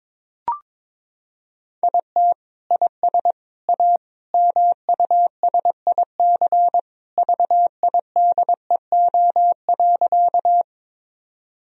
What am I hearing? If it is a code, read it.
Morse code, heard as a keyed tone, about 22 wpm, 700 Hz. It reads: IT IS A MUSIC VIDEO.
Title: Sentences from Top 1000 Words - Review Part 4 - 22wpm